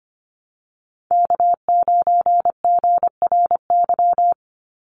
Morse code, heard as a keyed tone, about 25 wpm, 700 Hz. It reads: K9GRY
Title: USA Call Signs 25wpm